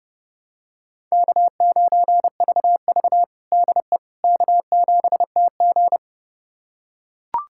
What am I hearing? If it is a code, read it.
Morse code, heard as a keyed tone, about 30 wpm, 700 Hz. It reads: K9VV DE K7TG
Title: USA Callsign DE Callsign - 4 Characters - 30wpm